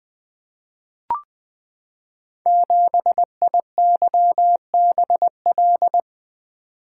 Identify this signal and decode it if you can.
Morse code, heard as a keyed tone, about 20 wpm, 700 Hz. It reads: 7IYBL
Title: Code Groups - 5 Letter:Numbers - 20wpm